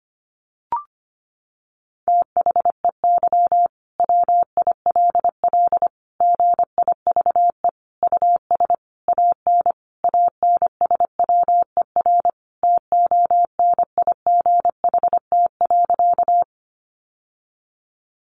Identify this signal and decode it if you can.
Morse code, heard as a keyed tone, about 25 wpm, 700 Hz. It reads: THEY WILL GIVE US AN ANSWER TONIGHT.
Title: Sentences from Top 600 Words 25wpm